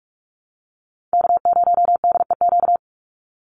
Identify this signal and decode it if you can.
Morse code, heard as a keyed tone, about 44 wpm, 700 Hz. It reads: K0DEQ
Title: Call Signs Repeated 2x - 45wpm